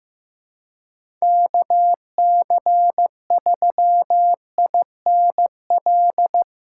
Morse code, heard as a keyed tone, about 15 wpm, 700 Hz. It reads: KC3INL
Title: USA Call Signs - 6 Characters - 6x Character Spacing - 15wpm